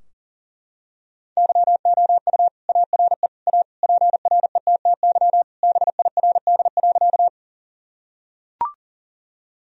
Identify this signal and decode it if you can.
Morse code, heard as a keyed tone, about 40 wpm, 700 Hz. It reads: YOU ARE A PRETTY BIRD.
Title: Sentences from Top 1000 Words - Review Part 2 - 40wpm